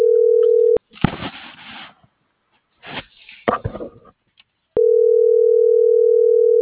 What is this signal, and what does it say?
Ringback tone (North America), 440 + 480 Hz.